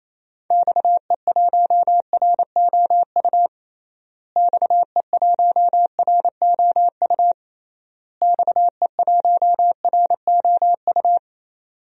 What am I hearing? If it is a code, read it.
Morse code, heard as a keyed tone, about 28 wpm, 700 Hz. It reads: XE1ROU XE1ROU XE1ROU
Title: DXCC-100 Call Signs - 6 Characters - Repeated 3x - 28wpm